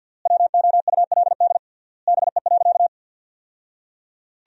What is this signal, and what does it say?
Morse code, heard as a keyed tone, about 50 wpm, 700 Hz. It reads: WOULD BE.
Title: 3x Word-Spacing - Sentences from Top 200 Words - 50wpm